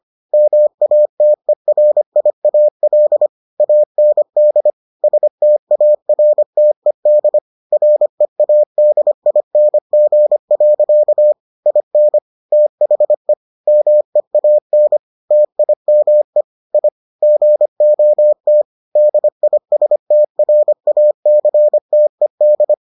Morse code, heard as a keyed tone, about 25 wpm, 600 Hz. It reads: MATERIAL AND STARTED READING. IN THE MEAN TIME I GOT DISTRACTED